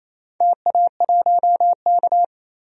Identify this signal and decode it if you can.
Morse code, heard as a keyed tone, about 27 wpm, 690 Hz. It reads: TA1K
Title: DXCC-100 Call Signs - 4 Characters - 28wpm